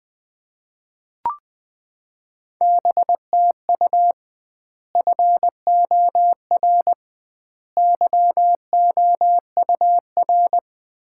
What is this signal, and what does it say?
Morse code, heard as a keyed tone, about 20 wpm, 700 Hz. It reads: BTU FOR YOUR
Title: Common QSO Phrases - 2x Word-Spacing - 20wpm